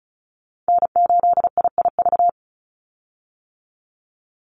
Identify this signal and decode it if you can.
Morse code, heard as a keyed tone, about 34 wpm, 700 Hz. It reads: N8IIV
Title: USA Call Signs 35wpm